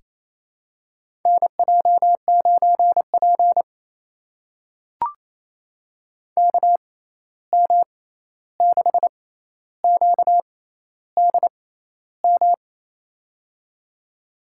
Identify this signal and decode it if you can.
Morse code, heard as a keyed tone, about 28 wpm, spaced as for 13 wpm, 700 Hz. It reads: NJ9P KM6QDM
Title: USA Call Signs - 6x Character Spacing - 28wpm